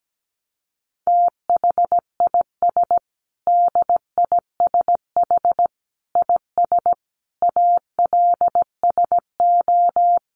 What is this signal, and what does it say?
Morse code, heard as a keyed tone, about 17 wpm, 700 Hz. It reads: THIS DISH IS ALSO
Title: Sentences from Top 1300 Words 17wpm